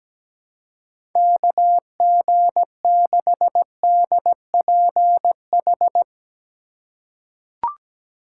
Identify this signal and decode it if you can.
Morse code, heard as a keyed tone, about 17 wpm, 700 Hz. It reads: KG6DPH
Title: USA Call Signs - 3x Character Spacing - 17wpm